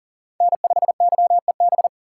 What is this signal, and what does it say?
Morse code, heard as a keyed tone, about 40 wpm, 690 Hz. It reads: N5YEB
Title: USA Call Signs - 4x Character Spacing - 40wpm